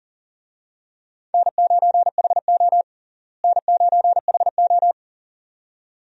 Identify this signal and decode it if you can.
Morse code, heard as a keyed tone, about 40 wpm, 700 Hz. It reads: N9HO N9HO
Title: USA Call Signs - 4 Characters - Repeated 2x - 40wpm